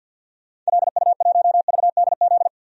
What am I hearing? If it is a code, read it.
Morse code, heard as a keyed tone, about 51 wpm, 700 Hz. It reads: LU1VDZ
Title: DXCC-100 Call Signs - 6 Characters - 50wpm